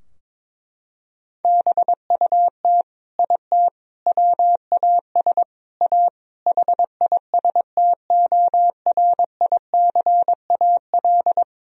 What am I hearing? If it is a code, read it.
Morse code, heard as a keyed tone, about 21 wpm, 700 Hz. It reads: BUT IT WAS A HISTORICAL
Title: Sentences from Top 1000 Words - Review Part 4 - 22wpm